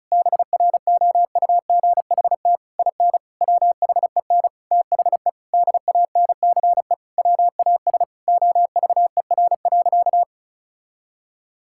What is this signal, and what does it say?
Morse code, heard as a keyed tone, about 36 wpm, 700 Hz. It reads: BROUGHT IN WHEN THE DANCE WAS OVER.
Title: Sentences from Top 1000 Words - Review Part 6 - 35wpm